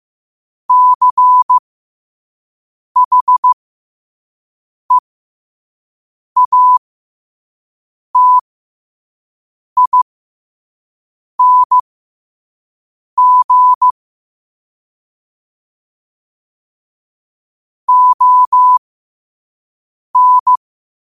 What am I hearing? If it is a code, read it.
Morse code, heard as a keyed tone, about 15 wpm, spaced as for 5 wpm, 1000 Hz. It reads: CHEATING ON
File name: qotdfiles / 5wpm